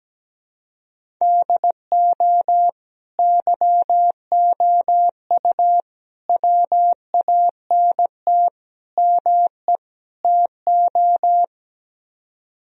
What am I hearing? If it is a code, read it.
Morse code, heard as a keyed tone, about 17 wpm, 700 Hz. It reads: DO YOU WANT ME TO
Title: Speed-Racing - Sets of 5 Words 17wpm